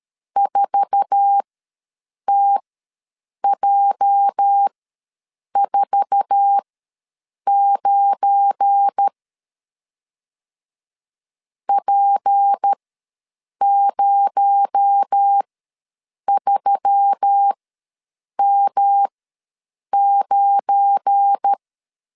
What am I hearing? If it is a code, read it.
Morse code, heard as a keyed tone, about 13 wpm, spaced as for 7 wpm, 800 Hz. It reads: 4TJ49 P03M9
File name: LEZIONI / 8 YJ45 / 8 YJ4 ws 12wpm 3249